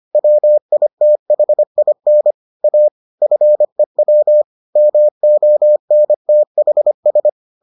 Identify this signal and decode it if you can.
Morse code, heard as a keyed tone, about 25 wpm, 600 Hz. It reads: WITHIN A FEW MONTHS